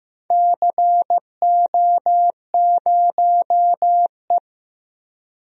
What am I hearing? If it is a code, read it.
Morse code, heard as a keyed tone, about 15 wpm, 700 Hz. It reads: CO0E
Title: DXCC-100 Call Signs - 4 Characters - Repeated 3x - 15wpm